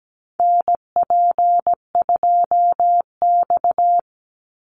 Morse code, heard as a keyed tone, about 17 wpm, 700 Hz. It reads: NP2X
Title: Call Signs - 17wpm